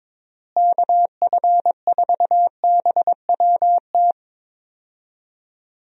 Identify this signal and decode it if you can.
Morse code, heard as a keyed tone, about 22 wpm, 700 Hz. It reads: KF4BWT
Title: USA Call Signs - 6 Characters - Repeated 3x - 22wpm